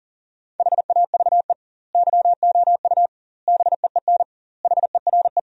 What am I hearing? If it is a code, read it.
Morse code, heard as a keyed tone, about 40 wpm, 700 Hz. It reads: HAVE YOU BEEN HERE
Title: 2x Word-Spacing - Sentences from Top 200 Words - 40wpm